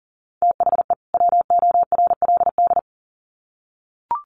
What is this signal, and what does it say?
Morse code, heard as a keyed tone, about 40 wpm, 700 Hz. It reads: THE WORLD
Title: Sets of 2 Words 40wpm